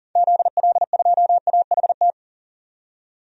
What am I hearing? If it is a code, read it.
Morse code, heard as a keyed tone, about 40 wpm, 700 Hz. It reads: ZL2AHT
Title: DXCC-100 Call Signs - 6 Characters - 40wpm